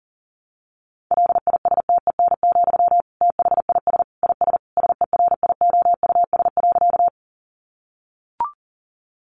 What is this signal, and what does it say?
Morse code, heard as a keyed tone, about 40 wpm, 700 Hz. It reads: LISTEN, THIS IS SERIOUS.
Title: Sentences from Top 1100 Words 40wpm